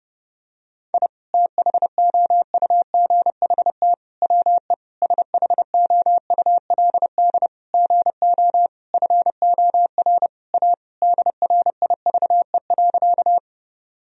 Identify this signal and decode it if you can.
Morse code, heard as a keyed tone, about 32 wpm, 700 Hz. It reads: I THOUGHT WE SHOULD GO FOR A DRIVE.
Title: Sentences from Top 1000 Words - Review Part 2 - 30wpm